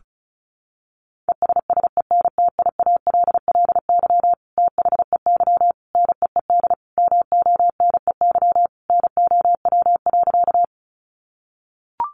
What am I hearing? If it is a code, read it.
Morse code, heard as a keyed tone, about 34 wpm, 700 Hz. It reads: ESSENTIALLY THEY NEED MONEY NOW.
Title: Sentences from Top 1300 Words 35wpm